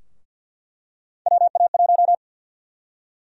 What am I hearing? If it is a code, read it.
Morse code, heard as a keyed tone, about 48 wpm, 700 Hz. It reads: WA1
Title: USA Call Sign Prefixes - 50wpm